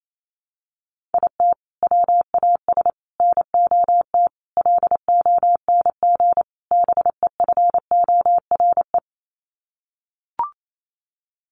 Morse code, heard as a keyed tone, about 28 wpm, 700 Hz. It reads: IT WAS NOT LONG BEFORE
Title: Sets of 5 Words Encore2 28wpm